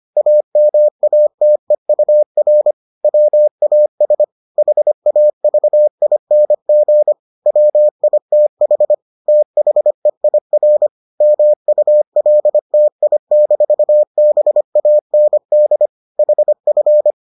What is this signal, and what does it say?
Morse code, heard as a keyed tone, about 25 wpm, 600 Hz. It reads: AMATEUR WAS HAVING WITH THEIR MULTI-BAND HF